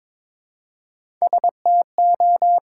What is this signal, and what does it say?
Morse code, heard as a keyed tone, about 23 wpm, 700 Hz. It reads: STO
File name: 5 Letter Words - First 3 Letters Given - 22wpm